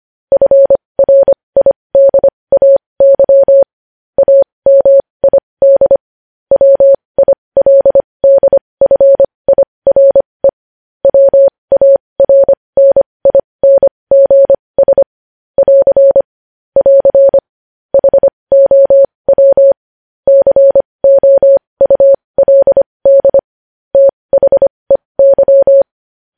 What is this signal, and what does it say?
Morse code, heard as a keyed tone, about 25 wpm, spaced as for 20 wpm, 550 Hz. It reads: FRIDAY AMID WILDFIRE WARNINGS <AR> <AR> HOW COULD THEY
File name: podcast / News headlines-1x-25x20